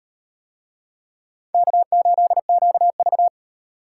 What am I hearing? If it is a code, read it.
Morse code, heard as a keyed tone, about 36 wpm, 700 Hz. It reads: K8QV